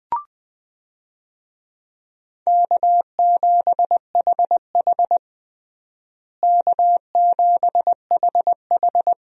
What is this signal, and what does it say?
Morse code, heard as a keyed tone, about 21 wpm, 700 Hz. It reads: K7HH K7HH
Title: USA Call Signs - 4 Characters - Repeated 2x - 20wpm